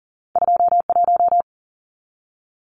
Morse code, heard as a keyed tone, about 41 wpm, 700 Hz. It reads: 21
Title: Numbers - 2-Digit 40wpm